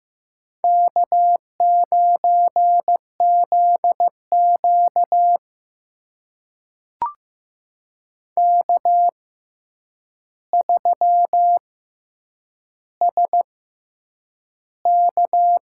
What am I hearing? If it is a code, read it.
Morse code, heard as a keyed tone, about 15 wpm, 700 Hz. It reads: K 9 Z Q K 3 S K
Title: USA Call Signs - 4 Characters - 6x Character Spacing - 15wpm